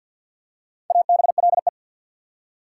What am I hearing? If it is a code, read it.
Morse code, heard as a keyed tone, about 51 wpm, 700 Hz. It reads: ABLE
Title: Four Letter Words - 50wpm